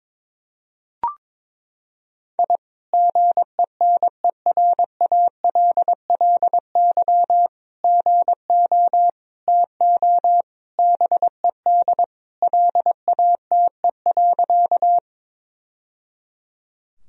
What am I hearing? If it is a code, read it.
Morse code, heard as a keyed tone, about 22 wpm, 700 Hz. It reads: I GENERALLY GO TO BED LATE.
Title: Sentences from Top 1000 Words - Review Part 4 - 22wpm